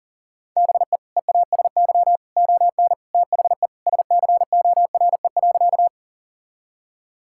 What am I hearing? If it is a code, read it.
Morse code, heard as a keyed tone, about 40 wpm, 700 Hz. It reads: BE EASY ON THE SCORE.
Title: Sentences from Top 1600 Words 40wpm